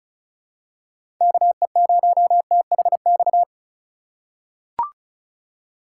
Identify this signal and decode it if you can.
Morse code, heard as a keyed tone, about 34 wpm, 700 Hz. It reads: KE0THX
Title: USA Call Signs - 6 Characters - 2x Character Spacing - 35wpm